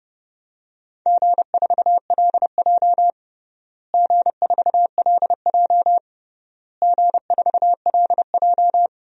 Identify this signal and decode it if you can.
Morse code, heard as a keyed tone, about 30 wpm, 700 Hz. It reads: G4LJ G4LJ G4LJ
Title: DXCC-100 Call Signs - 4 Characters - Repeated 3x - 30wpm